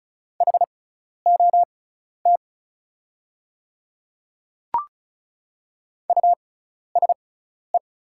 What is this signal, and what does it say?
Morse code, heard as a keyed tone, about 36 wpm, spaced as for 12 wpm, 700 Hz. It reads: HOT USE
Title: 6x Character Spacing - Top 100 Words - 35wpm